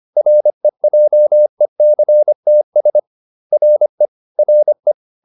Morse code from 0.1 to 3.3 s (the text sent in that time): REJECTS